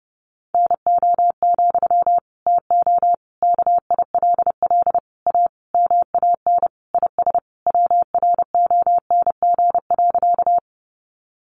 Morse code, heard as a keyed tone, about 30 wpm, 700 Hz. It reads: NO, TO KILL A MAN IS WRONG.
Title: Sentences from Top 1400 Words 30wpm